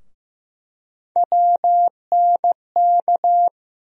Morse code, heard as a keyed tone, about 15 wpm, 700 Hz. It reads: WNK
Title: USA Call Sign Postfixes - 3 Letters 15wpm